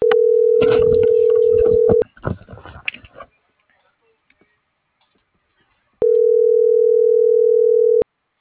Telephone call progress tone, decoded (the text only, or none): ringback tone